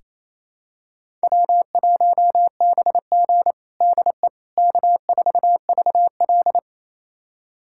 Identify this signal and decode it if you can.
Morse code, heard as a keyed tone, about 28 wpm, 700 Hz. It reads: W1BG DE K4VL
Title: USA Callsign DE Callsign - 4 Characters - 28wpm